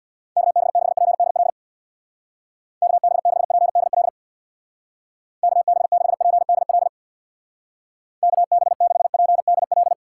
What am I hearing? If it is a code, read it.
Morse code, heard as a keyed tone, about 62 wpm, 700 Hz. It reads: KB6PDL KB6PDL KB6PDL KB6PDL
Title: Speed Racing - USA Call Signs 50wpm